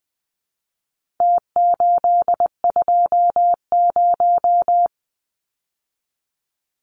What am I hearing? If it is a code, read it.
Morse code, heard as a keyed tone, about 20 wpm, 700 Hz. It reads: T820
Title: Numbers - Contest Style - 4-Digit 20wpm